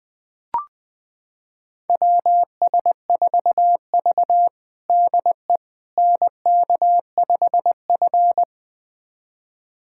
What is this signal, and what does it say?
Morse code, heard as a keyed tone, about 20 wpm, 700 Hz. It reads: WS4V DE NK5F
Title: USA Callsign DE Callsign - 4 Characters - 20wpm